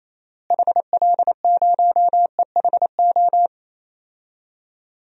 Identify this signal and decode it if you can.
Morse code, heard as a keyed tone, about 28 wpm, 700 Hz. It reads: HL0EHO
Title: DXCC-100 Call Signs - 6 Characters - Repeated 3x - 28wpm